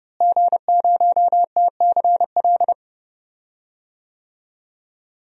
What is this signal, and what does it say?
Morse code, heard as a keyed tone, about 30 wpm, 700 Hz. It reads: G0TCL
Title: DXCC-100 Call Signs - 5 Characters - Repeated 2x - 30wpm